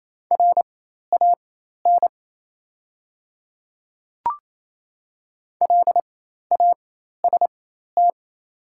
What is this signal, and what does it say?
Morse code, heard as a keyed tone, about 28 wpm, spaced as for 13 wpm, 700 Hz. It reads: RAN LAST
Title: 4x Character Spacing - Top 500 Words - 28wpm